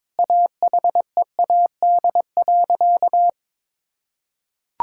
Morse code, heard as a keyed tone, about 22 wpm, 700 Hz. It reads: AHEAD.
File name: Sentences with 7-Word Phrases - 22wpm